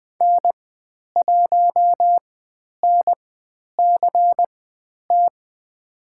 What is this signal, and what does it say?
Morse code, heard as a keyed tone, about 20 wpm, spaced as for 10 wpm, 700 Hz. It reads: N1NCT
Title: 095 Lesson 16g - Letter c - Call Signs 20wpm